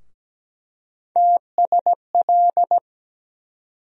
Morse code, heard as a keyed tone, about 17 wpm, 700 Hz. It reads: TSL
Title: USA Call Sign Postfixes - 3 Letters 17wpm